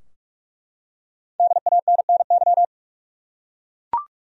Morse code, heard as a keyed tone, about 43 wpm, 700 Hz. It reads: DANNY
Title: USA Common Names - Max 5 Letters - 45wpm